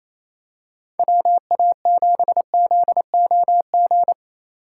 Morse code, heard as a keyed tone, about 28 wpm, 700 Hz. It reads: WA7ZOG
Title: USA Call Signs - 6 Characters - 8x Character Spacing - 28wpm